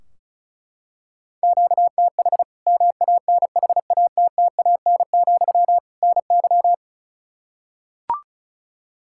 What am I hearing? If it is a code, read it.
Morse code, heard as a keyed tone, about 34 wpm, 700 Hz. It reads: QTH MANHATTAN, NY
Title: Practice QSO - Exchange 1 - Beginning - 35wpm